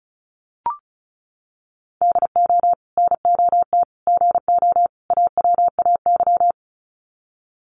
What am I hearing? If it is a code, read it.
Morse code, heard as a keyed tone, about 34 wpm, 700 Hz. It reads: DO NOT GO AWAY
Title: Sets of 4 Words 35wpm